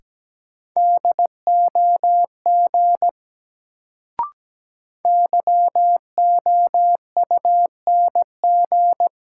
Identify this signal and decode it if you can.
Morse code, heard as a keyed tone, about 17 wpm, 700 Hz. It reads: DOG YOUNG
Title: Sentences from Top 400 Words - New Word Drill - 17wpm